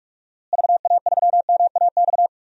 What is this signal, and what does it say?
Morse code, heard as a keyed tone, about 47 wpm, 700 Hz. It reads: VA3MAX
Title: DXCC-100 Call Signs - 6 Characters - Repeated 2x - 45wpm